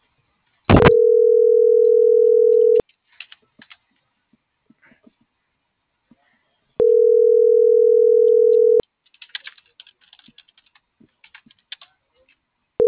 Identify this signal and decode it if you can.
Ringback tone (North America), 440 + 480 Hz.
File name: CENTRAL TERRESTRE / out erick.sanjuan